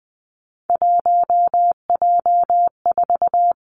Morse code, heard as a keyed tone, about 20 wpm, 700 Hz. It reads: 1J4
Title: Three Challenging Letter-Number Combinations 20wpm